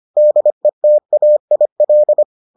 Morse code, heard as a keyed tone, about 25 wpm, 600 Hz. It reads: DETAIL